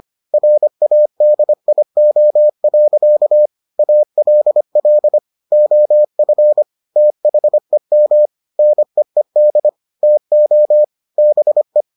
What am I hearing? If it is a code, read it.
Morse code, heard as a keyed tone, about 25 wpm, 600 Hz. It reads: RADIO. ALL OF THEM NEED TO BE